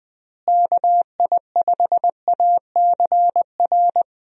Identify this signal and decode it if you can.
Morse code, heard as a keyed tone, about 20 wpm, 700 Hz. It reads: KI5ACR
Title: USA Call Signs - 6 Characters - Repeated 2x - 20wpm